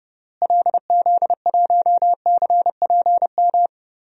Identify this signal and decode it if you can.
Morse code, heard as a keyed tone, about 30 wpm, 700 Hz. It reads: LZ1CPM
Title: DXCC-100 Call Signs - 6 Characters - Repeated 3x - 30wpm